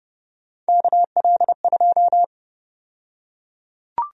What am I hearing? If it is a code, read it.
Morse code, heard as a keyed tone, about 30 wpm, 700 Hz. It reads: KL2
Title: USA Call Sign Prefixes - 30wpm